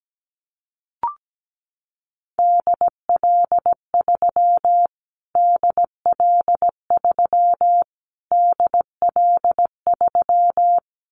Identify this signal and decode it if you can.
Morse code, heard as a keyed tone, about 17 wpm, 700 Hz. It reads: DL3 DL3 DL3
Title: Callsign Prefixes for Top 25 DXCC - Repeated 3x - 17wpm